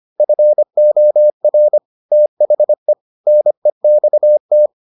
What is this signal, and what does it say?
Morse code, heard as a keyed tone, about 25 wpm, 600 Hz. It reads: FOR THE NEXT